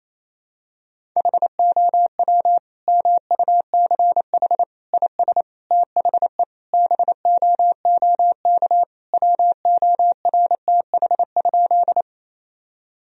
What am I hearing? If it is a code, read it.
Morse code, heard as a keyed tone, about 28 wpm, 700 Hz. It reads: HOW MUCH IS THE BOOK WORTH?
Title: Sentences from Top 1000 Words - Review Part 1 - 28wpm